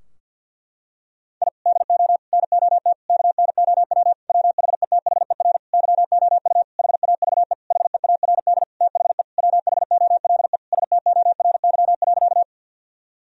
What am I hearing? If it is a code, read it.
Morse code, heard as a keyed tone, about 46 wpm, 700 Hz. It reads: I DO NOT KNOW WHETHER YOU HAVE HEARD THE WHOLE STORY.
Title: Sentences from Top 1000 Words - Review Part 3 - 50wpm